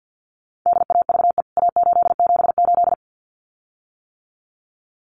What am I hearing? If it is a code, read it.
Morse code, heard as a keyed tone, about 50 wpm, 700 Hz. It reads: DAVE A878
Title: CWT Exchange 50wpm